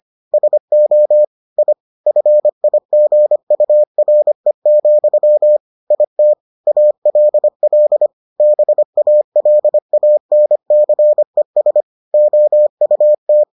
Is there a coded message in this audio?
Yes: Morse code from 0.3 to 13.4 s, heard as a keyed tone, about 25 wpm, 600 Hz. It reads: SO I FIGURE, IT ALL BALANCES OUT